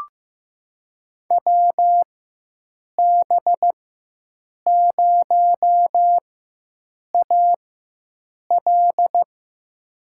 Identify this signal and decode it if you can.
Morse code, heard as a keyed tone, about 15 wpm, spaced as for 7 wpm, 700 Hz. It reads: WB0AL
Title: USA Call Signs - 5 Characters - 4x Character Spacing - 15wpm